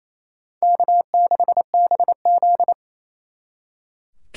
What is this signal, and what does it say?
Morse code, heard as a keyed tone, about 37 wpm, 700 Hz. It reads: K6BZ